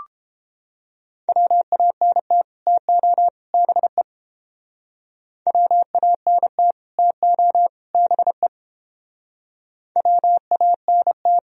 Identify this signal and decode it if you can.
Morse code, heard as a keyed tone, about 31 wpm, 700 Hz. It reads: WANT TO BE WANT TO BE WANT